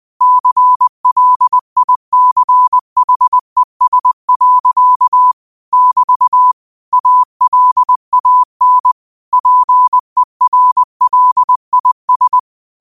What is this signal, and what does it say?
Morse code, heard as a keyed tone, about 20 wpm, 1000 Hz. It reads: CLICHES. <BT> ALAN PERLIS
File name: qotdfiles / 20wpm